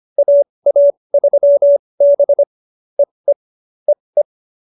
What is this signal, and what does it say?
Morse code, heard as a keyed tone, about 25 wpm, spaced as for 20 wpm, 570 Hz. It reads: AA3B EE EE
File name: QSO302 20